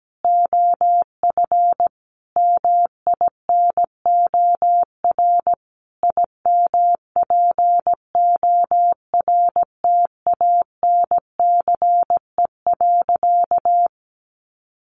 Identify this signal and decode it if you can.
Morse code, heard as a keyed tone, about 17 wpm, 700 Hz. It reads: OF MINOR IMPORTANCE.
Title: Sentences from Top 1500 Words 17wpm